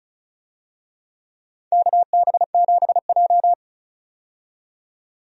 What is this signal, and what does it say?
Morse code, heard as a keyed tone, about 36 wpm, 700 Hz. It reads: KB7J
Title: USA Call Signs - 4 Characters - 35wpm